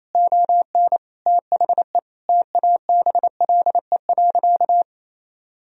Morse code, heard as a keyed tone, about 27 wpm, 700 Hz. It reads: ON THE TABLE.
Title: Sentences from Top 1000 Words - Review Part 2 - 28wpm